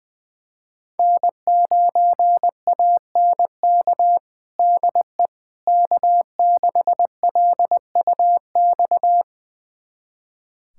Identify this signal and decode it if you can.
Morse code, heard as a keyed tone, about 20 wpm, 700 Hz. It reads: N9ANK DE K6LUX
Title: USA Callsign DE Callsign 20wpm